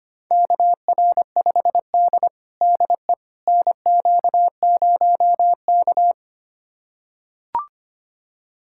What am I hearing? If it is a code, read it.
Morse code, heard as a keyed tone, about 25 wpm, 700 Hz. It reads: KR5D DE NQ0K
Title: USA Callsign DE Callsign - 4 Characters - 25wpm